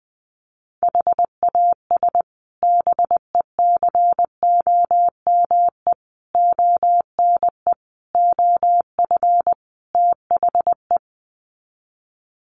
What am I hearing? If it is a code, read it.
Morse code, heard as a keyed tone, about 19 wpm, 700 Hz. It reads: HAS BECOME ONE OF THE
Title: Sets of 5 Words Encore 20wpm